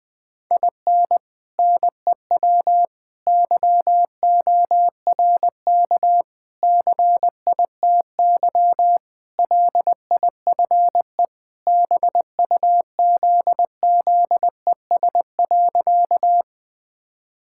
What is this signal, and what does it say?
Morse code, heard as a keyed tone, about 20 wpm, 700 Hz. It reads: IN NEW YORK CITY LIFE BUZZES.
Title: Sentences with 4-Word Phrases - Spoken Beginning - 20wpm